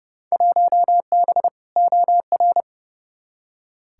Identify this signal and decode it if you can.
Morse code, heard as a keyed tone, about 29 wpm, 700 Hz. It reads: 1B OR
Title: Field Day Exchanges - 30wpm